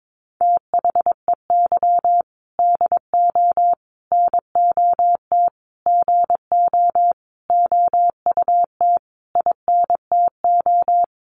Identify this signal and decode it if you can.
Morse code, heard as a keyed tone, about 22 wpm, 700 Hz. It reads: THEY DO NOT GO OUT INTO
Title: Sentences from Top 700 Words 22wpm